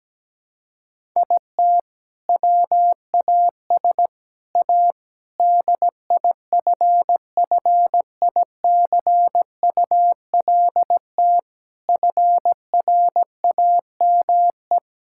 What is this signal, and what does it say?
Morse code, heard as a keyed tone, about 17 wpm, 700 Hz. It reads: IT WAS A DIFFICULT FRAME
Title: Sentences from Top 1000 Words - Review Part 6 - 17wpm